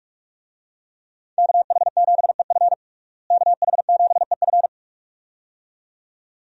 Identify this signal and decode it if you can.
Morse code, heard as a keyed tone, about 46 wpm, 700 Hz. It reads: KH7EF KH7EF
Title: USA Call Signs - 5 Characters - Repeated 2x - 45wpm